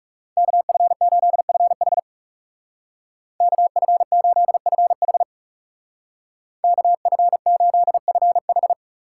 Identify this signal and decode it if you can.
Morse code, heard as a keyed tone, about 36 wpm, 700 Hz. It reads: KF8FH KF8FH KF8FH
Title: Speed Racing - USA 2x2 Call Signs 30wpm